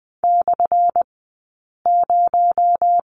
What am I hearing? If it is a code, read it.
Morse code, heard as a keyed tone, about 20 wpm, spaced as for 8 wpm, 690 Hz. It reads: /0
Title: 217 Lesson 35d - Number 0 - Two Characters 20wpm